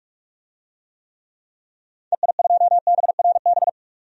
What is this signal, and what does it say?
Morse code, heard as a keyed tone, about 43 wpm, 700 Hz. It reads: EI2BRB
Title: DXCC-100 Call Signs - 6 Characters - 45wpm